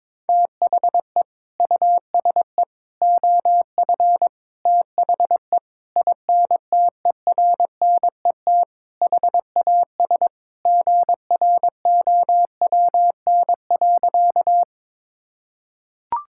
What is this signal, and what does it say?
Morse code, heard as a keyed tone, about 22 wpm, 700 Hz. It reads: THE USE OF THE INTERNET HAS GROWN.
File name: practice / 22wpm / Sentences with 4-Word Phrases - 22wpm